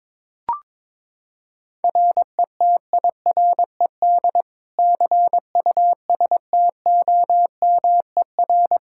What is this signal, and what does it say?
Morse code, heard as a keyed tone, about 21 wpm, 700 Hz. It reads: RETIRED CUSTOMER
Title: Practice QSO - Exchange 3 - More Info - 22wpm